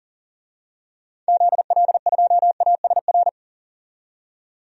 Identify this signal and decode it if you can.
Morse code, heard as a keyed tone, about 40 wpm, 700 Hz. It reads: ZL2ASR
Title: DXCC-100 Call Signs - 6 Characters - 40wpm